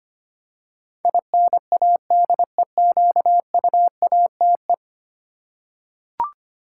Morse code, heard as a keyed tone, about 25 wpm, 700 Hz. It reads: INADEQUATE
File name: 10 Letter Words - First 3 Letters Given - 25wpm